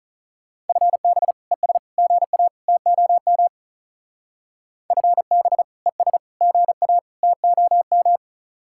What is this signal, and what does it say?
Morse code, heard as a keyed tone, about 41 wpm, 700 Hz. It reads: FB ES GA TOM FB ES GA TOM
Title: Speed-Racing Common QSO Phrases - 35wpm